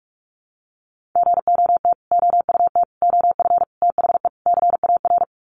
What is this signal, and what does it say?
Morse code, heard as a keyed tone, about 44 wpm, 700 Hz. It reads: GOT OUT OF THE CAR
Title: Sets of 5 Words Encore 45wpm